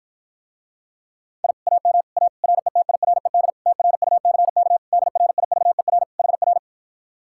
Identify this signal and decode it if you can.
Morse code, heard as a keyed tone, about 54 wpm, 700 Hz. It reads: I AM A RETIRED TRUCK DRIVER HR